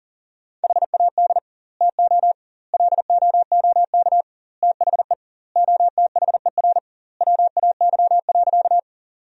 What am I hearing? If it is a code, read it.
Morse code, heard as a keyed tone, about 40 wpm, 700 Hz. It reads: HAD TO LOOK THE OTHER WAY.
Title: 2x Word-Spacing - Sentences from Top 100 Words - 40wpm